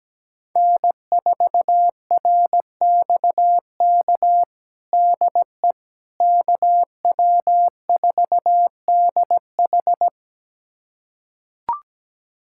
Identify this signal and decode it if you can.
Morse code, heard as a keyed tone, about 17 wpm, 700 Hz. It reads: N4RXK DE KW4DH
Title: USA Callsign DE Callsign - 5 Characters - 17wpm